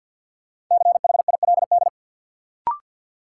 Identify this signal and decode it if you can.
Morse code, heard as a keyed tone, about 52 wpm, 700 Hz. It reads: CHILD
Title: Sentences from Top 800 Words - New Word Drill - 50wpm